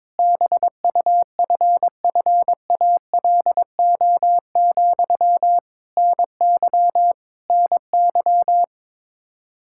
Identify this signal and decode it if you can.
Morse code, heard as a keyed tone, about 22 wpm, 700 Hz. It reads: BUFFALO, NY NY
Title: K1USN SST QSO Practice 22wpm